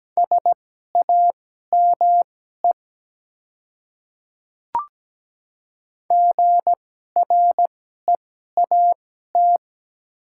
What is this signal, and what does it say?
Morse code, heard as a keyed tone, about 17 wpm, spaced as for 12 wpm, 700 Hz. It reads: SAME GREAT
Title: 2x Character Spacing - Top 300 Words - 17wpm